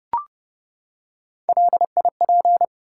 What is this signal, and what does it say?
Morse code, heard as a keyed tone, about 30 wpm, 700 Hz. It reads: LIP
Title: Three Letter Words - 30wpm